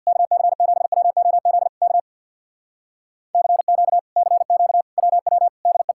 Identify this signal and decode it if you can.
Morse code, heard as a keyed tone, about 64 wpm, 700 Hz. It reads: KC7WOZ K CQ CQ WW DE